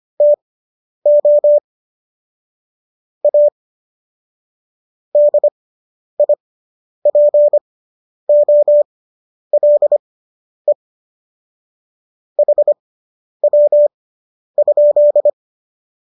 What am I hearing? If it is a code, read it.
Morse code, heard as a keyed tone, about 25 wpm, spaced as for 10 wpm, 590 Hz. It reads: TO A DIPOLE HW?